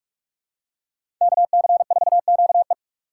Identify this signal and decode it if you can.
Morse code, heard as a keyed tone, about 46 wpm, 700 Hz. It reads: KC4QE
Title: USA Call Signs - 5 Characters - 2x Character Spacing - 45wpm